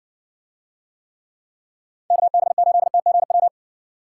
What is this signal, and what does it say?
Morse code, heard as a keyed tone, about 60 wpm, 700 Hz. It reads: KB8TGW